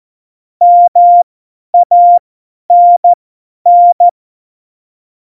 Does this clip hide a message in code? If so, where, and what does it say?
Yes: Morse code from 0.6 to 4.1 s, heard as a keyed tone, about 14 wpm, spaced as for 10 wpm, 700 Hz. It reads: MANN